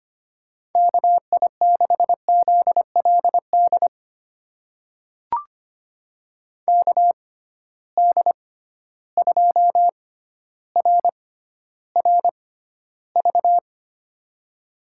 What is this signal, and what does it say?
Morse code, heard as a keyed tone, about 25 wpm, spaced as for 13 wpm, 700 Hz. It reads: KI6ZLD KD2RRV
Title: USA Call Signs - 6x Character Spacing - 25wpm